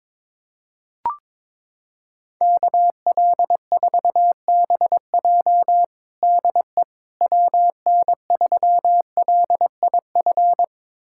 Morse code, heard as a keyed tone, about 22 wpm, 700 Hz. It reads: KL4BJ DE WN3LIF
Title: USA Callsign DE Callsign 22wpm